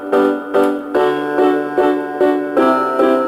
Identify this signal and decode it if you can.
There is no code.